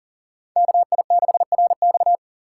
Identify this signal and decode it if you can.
Morse code, heard as a keyed tone, about 47 wpm, 700 Hz. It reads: KI6RX
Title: Speed Racing - USA 2x2 Call Signs 40wpm